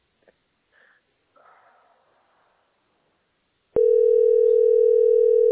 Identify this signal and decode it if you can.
Ringback tone (North America), 440 + 480 Hz.